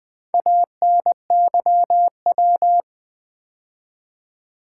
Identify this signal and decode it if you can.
Morse code, heard as a keyed tone, about 20 wpm, 700 Hz. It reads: ANYW
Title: Build A Word - 2 Letters At A Time - 8-Letter Words - 20wpm